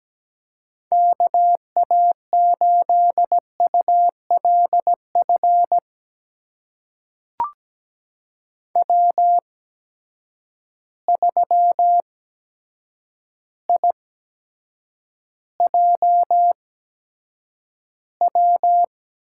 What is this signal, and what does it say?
Morse code, heard as a keyed tone, about 17 wpm, spaced as for 8 wpm, 700 Hz. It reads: KA8ULF W3IJW
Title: USA Call Signs - 8x Character Spacing - 17wpm